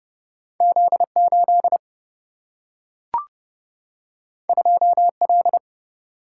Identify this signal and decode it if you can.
Morse code, heard as a keyed tone, about 30 wpm, 700 Hz. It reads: Z8 2L
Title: Single-Letter Single-Number 30wpm